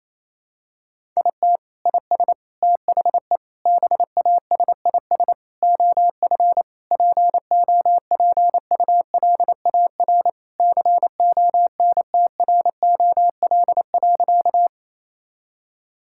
Morse code, heard as a keyed tone, about 28 wpm, 700 Hz. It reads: IT IS THE BASIS OF POPULAR CONTROL.
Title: Sentences from Top 1000 Words - Review Part 5 - 28wpm